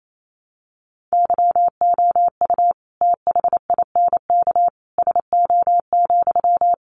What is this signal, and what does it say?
Morse code, heard as a keyed tone, about 28 wpm, 700 Hz. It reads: YOU THINK SO,
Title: Sentences from Top 200 Words 28wpm